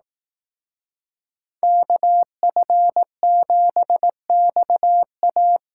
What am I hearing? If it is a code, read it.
Morse code, heard as a keyed tone, about 18 wpm, 700 Hz. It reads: KF7XA
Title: Speed Racing - USA 2x2 Call Signs 15wpm